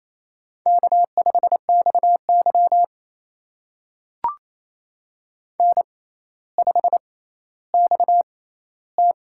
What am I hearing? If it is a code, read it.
Morse code, heard as a keyed tone, about 28 wpm, spaced as for 14 wpm, 700 Hz. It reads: K5XY N5XT
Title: USA Call Signs - 4 Characters - 6x Character Spacing - 28wpm